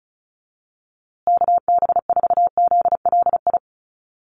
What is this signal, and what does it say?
Morse code, heard as a keyed tone, about 34 wpm, 700 Hz. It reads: KB4ZLI
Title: USA Call Signs Repeated 3x 35wpm